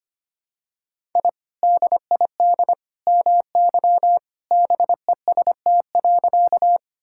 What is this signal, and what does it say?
Morse code, heard as a keyed tone, about 25 wpm, 700 Hz. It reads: I DID MY BEST.
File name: practice / 25wpm / Sentences with 2-Word Phrases - 25wpm